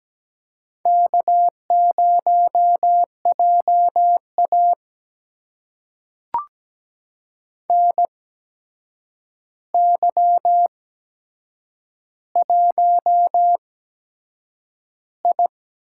Morse code, heard as a keyed tone, about 17 wpm, spaced as for 7 wpm, 700 Hz. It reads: K0JA NY1I
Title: USA Call Signs - 4 Characters - 8x Character Spacing - 17wpm